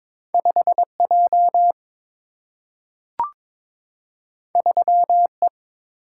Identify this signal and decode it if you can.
Morse code, heard as a keyed tone, about 22 wpm, 700 Hz. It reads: 5 J 3 E
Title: Single-Letter Single-Number 22wpm